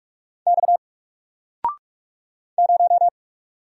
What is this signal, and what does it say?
Morse code, heard as a keyed tone, about 44 wpm, 700 Hz. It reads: X 0
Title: Single Letter-Number 45wpm